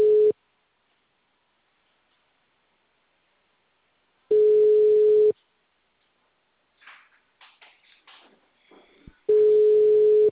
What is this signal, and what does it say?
Ringback tone (Europe, 425 Hz).